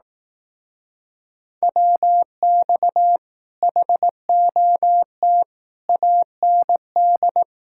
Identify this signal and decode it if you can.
Morse code, heard as a keyed tone, about 18 wpm, 700 Hz. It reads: WX HOT AND
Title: Speed-Racing Common QSO Phrases Encore - 15wpm